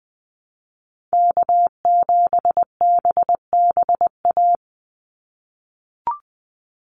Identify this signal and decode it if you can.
Morse code, heard as a keyed tone, about 20 wpm, 700 Hz. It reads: K7BBA
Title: USA Call Signs 20wpm